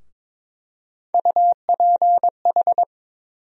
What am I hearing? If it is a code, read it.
Morse code, heard as a keyed tone, about 22 wpm, 700 Hz. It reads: UPH
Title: USA Call Sign Postfixes - 3 Letters 22wpm